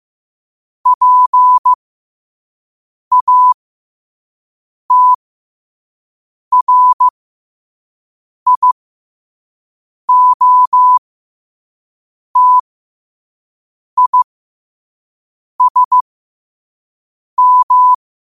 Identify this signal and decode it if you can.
Morse code, heard as a keyed tone, about 15 wpm, spaced as for 5 wpm, 1000 Hz. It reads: PATRIOTISM
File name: qotdfiles / 5wpm